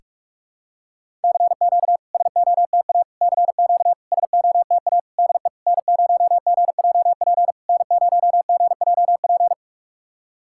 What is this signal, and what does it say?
Morse code, heard as a keyed tone, about 44 wpm, 690 Hz. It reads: CQ SOTA CQ SOTA DE N0GJP N0GJP
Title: SOTA QSO Practice 45wpm